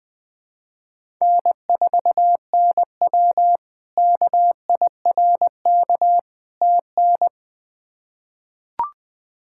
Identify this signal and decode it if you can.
Morse code, heard as a keyed tone, about 20 wpm, 700 Hz. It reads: N4NW KIRK TN
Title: K1USN SST QSO Practice - 20wpm